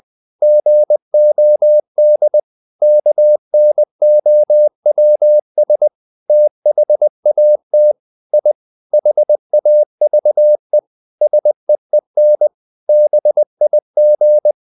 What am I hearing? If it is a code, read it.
Morse code, heard as a keyed tone, about 20 wpm, 600 Hz. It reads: GOD KNOWS THAT I HAVE SEEN BIG
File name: the old man and the sea part 2 20wpm